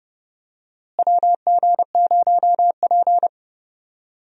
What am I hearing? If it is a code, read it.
Morse code, heard as a keyed tone, about 30 wpm, 700 Hz. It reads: WG0P
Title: USA Call Signs - 4 Characters - Repeated 3x - 30wpm